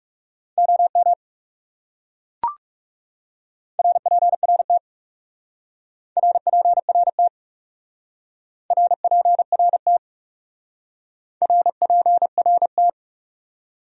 Morse code, heard as a keyed tone, about 36 wpm, 700 Hz. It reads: OM RPRT RPRT RPRT RPRT
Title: Speed-Racing Abbreviations - 30wpm